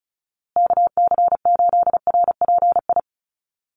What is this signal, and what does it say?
Morse code, heard as a keyed tone, about 34 wpm, 700 Hz. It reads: KC8RPI
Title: USA Call Signs Repeated 2x 35wpm